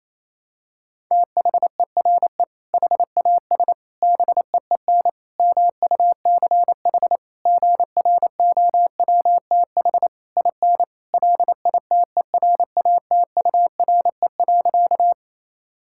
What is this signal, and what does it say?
Morse code, heard as a keyed tone, about 28 wpm, 700 Hz. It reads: THERE HAS BEEN MUCH GROWTH IN LITERATURE.
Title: Sentences from Top 1000 Words - Review Part 5 - 28wpm